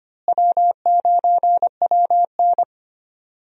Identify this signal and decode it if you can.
Morse code, heard as a keyed tone, about 25 wpm, 700 Hz. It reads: W9WN
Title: USA Call Signs - 4 Characters - Repeated 3x - 25wpm